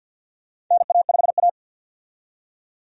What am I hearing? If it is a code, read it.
Morse code, heard as a keyed tone, about 46 wpm, 700 Hz. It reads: NA5A
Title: USA Call Signs - 4 Characters - 50wpm